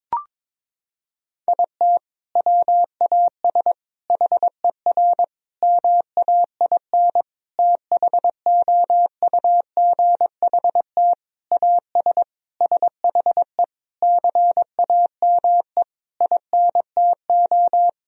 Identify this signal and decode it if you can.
Morse code, heard as a keyed tone, about 22 wpm, 700 Hz. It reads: IT WAS HER MAIN THOUGHT AS SHE CAME INTO
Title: Sentences from Top 1000 Words - Review Part 2 - 22wpm